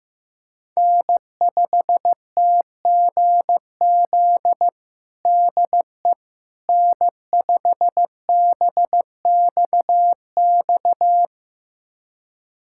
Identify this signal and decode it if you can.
Morse code, heard as a keyed tone, about 16 wpm, 700 Hz. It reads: N5TGZ DE N5BXX
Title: USA Callsign DE Callsign - 5 Characters - 15wpm